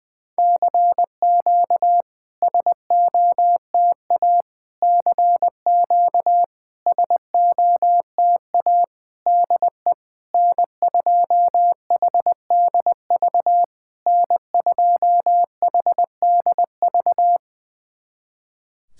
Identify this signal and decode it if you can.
Morse code, heard as a keyed tone, about 21 wpm, 690 Hz. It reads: CQ SOTA CQ SOTA DE N2HDV N2HDV
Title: SOTA QSO Practice 20wpm